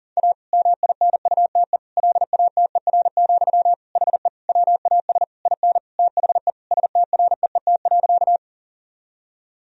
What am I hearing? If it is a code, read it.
Morse code, heard as a keyed tone, about 40 wpm, 700 Hz. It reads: A MINUTE LATER, HE WAS IN THE STREET.
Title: Sentences from Top 1000 Words - Review Part 4 - 40wpm